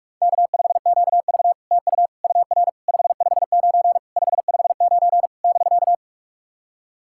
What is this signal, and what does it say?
Morse code, heard as a keyed tone, about 46 wpm, 700 Hz. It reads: K5QV TU UR 559 559 <BK>
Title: SOTA QSO Practice 45wpm